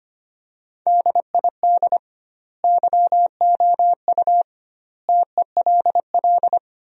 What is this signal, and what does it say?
Morse code, heard as a keyed tone, about 25 wpm, 700 Hz. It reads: DID YOU TELL
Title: 2x Word-Spacing - Sentences from Top 200 Words - 25wpm